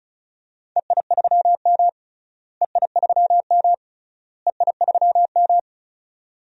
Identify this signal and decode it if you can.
Morse code, heard as a keyed tone, about 34 wpm, 700 Hz. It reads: EI3M EI3M EI3M
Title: DXCC-100 Call Signs - 4 Characters - Repeated 3x - 35wpm